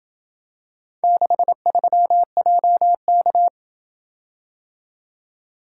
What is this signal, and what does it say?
Morse code, heard as a keyed tone, about 27 wpm, 700 Hz. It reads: 63JK